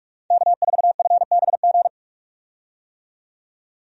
Morse code, heard as a keyed tone, about 46 wpm, 700 Hz. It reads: K4FBG
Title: USA Call Signs - 5 Characters - 45wpm